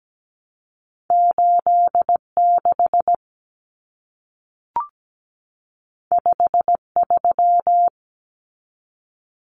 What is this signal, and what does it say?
Morse code, heard as a keyed tone, about 17 wpm, 700 Hz. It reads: 86 53
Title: Common Cut Numbers - 2-Digit 17wpm